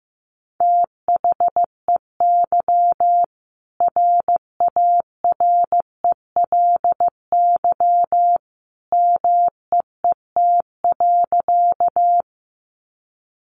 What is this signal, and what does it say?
Morse code, heard as a keyed tone, about 15 wpm, 700 Hz. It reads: THEY RARELY MEET.
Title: Sentences from Top 1400 Words 15wpm